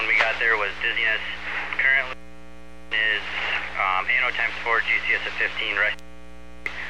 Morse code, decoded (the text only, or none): none